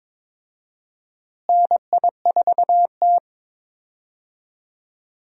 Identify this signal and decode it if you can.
Morse code, heard as a keyed tone, about 21 wpm, 700 Hz. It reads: NI4T
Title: USA Call Signs - 4 Characters - Repeated 2x - 22wpm